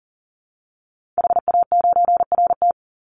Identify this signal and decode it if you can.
Morse code, heard as a keyed tone, about 40 wpm, 700 Hz. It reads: HA9RT
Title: Call Signs Repeated 3x - 40wpm